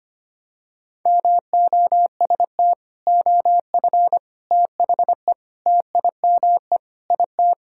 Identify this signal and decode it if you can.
Morse code, heard as a keyed tone, about 25 wpm, 700 Hz. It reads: MOST OF THE TIME IT
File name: practice / 25wpm / Sentences with 3-Word Phrases - 25wpm